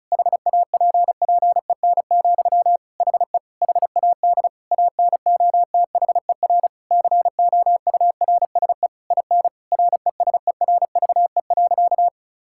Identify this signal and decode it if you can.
Morse code, heard as a keyed tone, about 36 wpm, 700 Hz. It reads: HAPPEN, HE HAD ANOTHER COURSE IN RESERVE.
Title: Sentences from Top 1000 Words - Review Part 6 - 35wpm